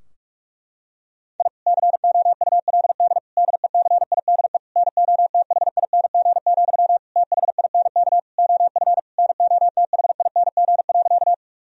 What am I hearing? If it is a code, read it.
Morse code, heard as a keyed tone, about 45 wpm, 700 Hz. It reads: I COULD DECIDE NOTHING, THINK OF NOTHING.
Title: Sentences from Top 1000 Words - Review Part 3 - 45wpm